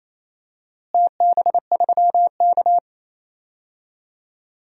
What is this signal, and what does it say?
Morse code, heard as a keyed tone, about 28 wpm, 700 Hz. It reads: TB3K
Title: DXCC-100 Call Signs - 4 Characters - Repeated 3x - 28wpm